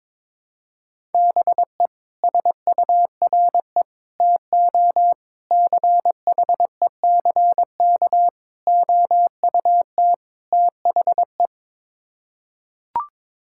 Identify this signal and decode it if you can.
Morse code, heard as a keyed tone, about 22 wpm, 700 Hz. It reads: BE SURE TO CHECK OUT THE
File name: practice / 22wpm / Sets of 6 Words - 22wpm